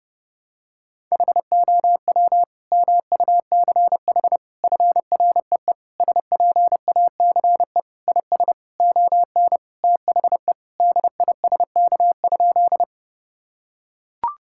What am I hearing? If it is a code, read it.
Morse code, heard as a keyed tone, about 30 wpm, 700 Hz. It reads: HOW MUCH FREE SPACE IS ON THE DISK?
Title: Sentences from Top 1000 Words - Review Part 6 - 30wpm